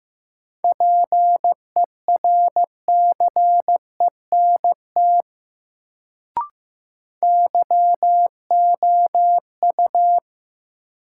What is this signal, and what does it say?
Morse code, heard as a keyed tone, about 15 wpm, 700 Hz. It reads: PERCENT YOU
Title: Sets of 2 Words - New Word Drill - 15wpm